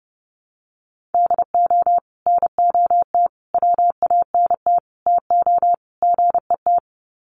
Morse code, heard as a keyed tone, about 30 wpm, 700 Hz. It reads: DO NOT WANT TO GET
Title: Sets of 5 Words 30wpm